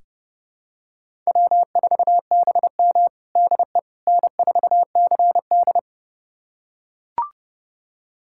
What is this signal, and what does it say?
Morse code, heard as a keyed tone, about 32 wpm, 700 Hz. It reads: W4BM DE N4CD
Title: USA Callsign DE Callsign - 4 Characters - 30wpm